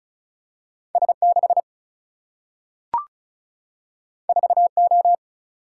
Morse code, heard as a keyed tone, about 36 wpm, 700 Hz. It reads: S 6 4 O
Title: Single-Letter Single-Number 35wpm